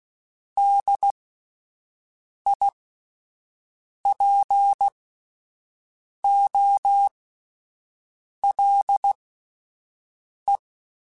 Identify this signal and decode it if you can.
Morse code, heard as a keyed tone, about 16 wpm, spaced as for 5 wpm, 800 Hz. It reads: DIPOLE
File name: fromSMs / FromSMs mp3s / Antennas@5